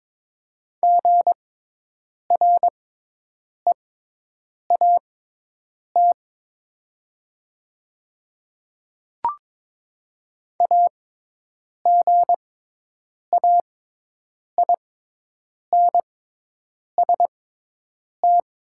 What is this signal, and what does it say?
Morse code, heard as a keyed tone, about 22 wpm, spaced as for 8 wpm, 700 Hz. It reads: GREAT AGAINST
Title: 6x Character Spacing - Top 500 Words - 22wpm